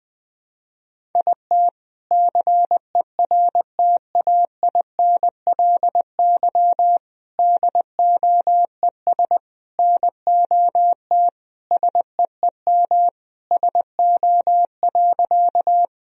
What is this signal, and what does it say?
Morse code, heard as a keyed tone, about 21 wpm, 700 Hz. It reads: IT CERTAINLY DOES NOT SEEM SO.
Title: Sentences from Top 1000 Words - Review Part 1 - 20wpm